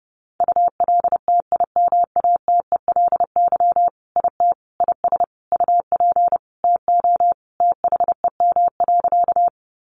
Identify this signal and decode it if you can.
Morse code, heard as a keyed tone, about 30 wpm, 700 Hz. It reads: ULTIMATELY IT IS UP TO THEM.
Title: Sentences from Top 1200 Words 30wpm